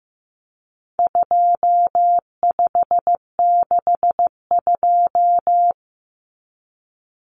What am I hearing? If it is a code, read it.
Morse code, heard as a keyed tone, about 15 wpm, 700 Hz. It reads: 2562
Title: Common Cut Numbers - 4-Digit 15wpm